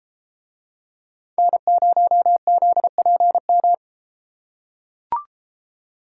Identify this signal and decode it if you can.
Morse code, heard as a keyed tone, about 32 wpm, 700 Hz. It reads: N0ZPM